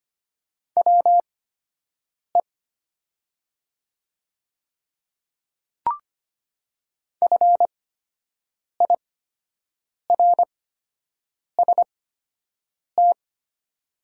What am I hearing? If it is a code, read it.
Morse code, heard as a keyed tone, about 25 wpm, spaced as for 7 wpm, 700 Hz. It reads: WE FIRST
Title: 8x Character Spacing - Top 200 Words - 25wpm